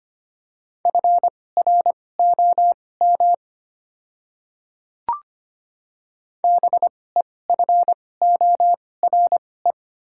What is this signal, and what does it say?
Morse code, heard as a keyed tone, about 25 wpm, spaced as for 18 wpm, 700 Hz. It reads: FROM BEFORE
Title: 2x Character Spacing - Top 200 Words - 25wpm